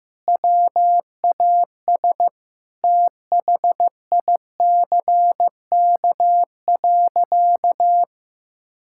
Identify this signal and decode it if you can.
Morse code, heard as a keyed tone, about 15 wpm, 700 Hz. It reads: WAS THICK.
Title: Sentences from Top 1600 Words 15wpm